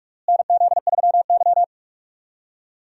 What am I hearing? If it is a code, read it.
Morse code, heard as a keyed tone, about 41 wpm, 700 Hz. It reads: NZ3Y
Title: USA Call Signs - 4 Characters - Repeated 2x - 45wpm